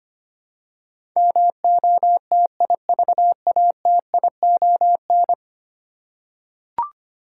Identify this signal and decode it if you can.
Morse code, heard as a keyed tone, about 25 wpm, 700 Hz. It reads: MOTIVATION
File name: 10 Letter Words - First 3 Letters Given - 25wpm